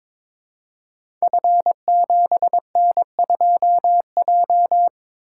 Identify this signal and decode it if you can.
Morse code, heard as a keyed tone, about 22 wpm, 700 Hz. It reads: F7N2J
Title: Code Groups - 5 Letter:Numbers - 22wpm